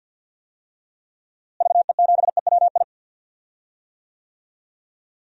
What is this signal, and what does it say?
Morse code, heard as a keyed tone, about 49 wpm, 700 Hz. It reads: VE7EWI
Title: DXCC-100 Call Signs - 6 Characters - Repeated 2x - 50wpm